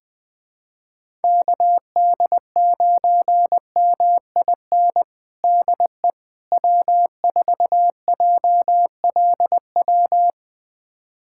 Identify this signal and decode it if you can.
Morse code, heard as a keyed tone, about 21 wpm, 700 Hz. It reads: KD9MIN DE W4JLW
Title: USA Callsign DE Callsign 20wpm